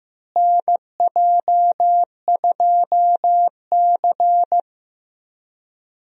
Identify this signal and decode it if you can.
Morse code, heard as a keyed tone, about 15 wpm, 700 Hz. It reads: NJ2C
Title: USA Call Signs - 4 Characters - Repeated 3x - 15wpm